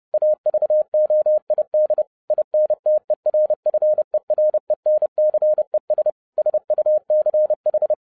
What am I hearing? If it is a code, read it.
Morse code, heard as a keyed tone, about 30 wpm, 610 Hz. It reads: AVOID INTERFERENCES SUCH
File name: 2011-07-mp3 / n40 EXCITING QRP CW OPERATION 30wpm